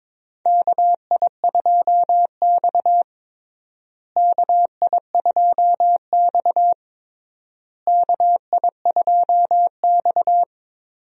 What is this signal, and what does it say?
Morse code, heard as a keyed tone, about 22 wpm, 700 Hz. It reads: KI2X KI2X KI2X
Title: USA Call Signs - 4 Characters - Repeated 3x - 22wpm